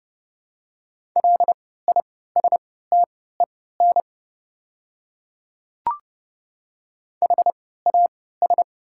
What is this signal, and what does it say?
Morse code, heard as a keyed tone, about 30 wpm, spaced as for 17 wpm, 700 Hz. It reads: LISTEN HAS
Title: 3x Character Spacing - Top 500 Words - 30wpm